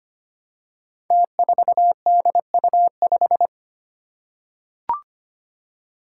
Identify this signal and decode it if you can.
Morse code, heard as a keyed tone, about 25 wpm, 700 Hz. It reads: T4DU5
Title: Five Challenging Letter-Number Combinations 25wpm